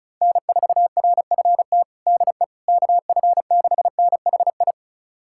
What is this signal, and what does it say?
Morse code, heard as a keyed tone, about 36 wpm, 700 Hz. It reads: N4RFT DE KF6NHI
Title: USA Callsign DE Callsign 35wpm